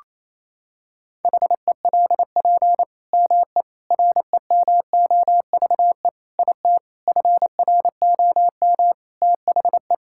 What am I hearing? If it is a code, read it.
Morse code, heard as a keyed tone, about 28 wpm, 700 Hz. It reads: HELP ME REMOVE IT FROM THE
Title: Sentences from Top 1000 Words - Review Part 5 - 28wpm